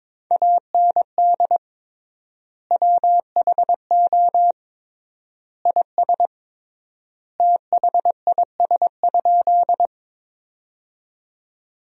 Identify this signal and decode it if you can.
Morse code, heard as a keyed tone, about 21 wpm, 700 Hz. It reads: AND WHO IS THIS?
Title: 3x Word-Spacing - Sentences from Top 100 Words - 22wpm